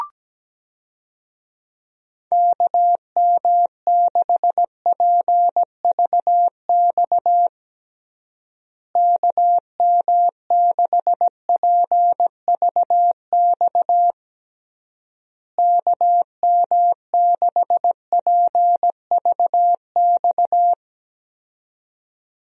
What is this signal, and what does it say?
Morse code, heard as a keyed tone, about 17 wpm, 700 Hz. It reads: KM6PVX KM6PVX KM6PVX
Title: USA Call Signs - 6 Characters - Repeated 3x - 17wpm